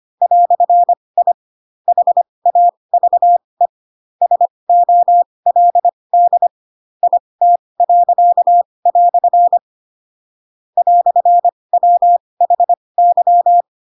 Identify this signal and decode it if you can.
Morse code, heard as a keyed tone, about 25 wpm, spaced as for 20 wpm, 700 Hz. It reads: "I HAVE SOLD IT." "WHY
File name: Pinocchio / Pinocchio ch8 15 25 ik1hge